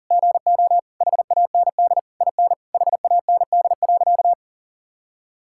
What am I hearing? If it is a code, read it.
Morse code, heard as a keyed tone, about 40 wpm, 700 Hz. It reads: GO HAND IN HAND.
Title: Sentences from Top 1000 Words - Review Part 6 - 40wpm